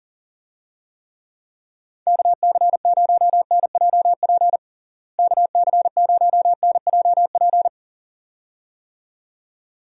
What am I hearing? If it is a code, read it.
Morse code, heard as a keyed tone, about 37 wpm, 700 Hz. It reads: KC0NJP KC0NJP
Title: USA Call Signs - 6 Characters - Repeated 2x - 40wpm